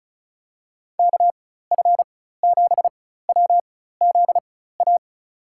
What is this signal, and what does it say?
Morse code, heard as a keyed tone, about 36 wpm, spaced as for 16 wpm, 700 Hz. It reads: KF7WZA
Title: USA Call Signs - 4x Character Spacing - 35wpm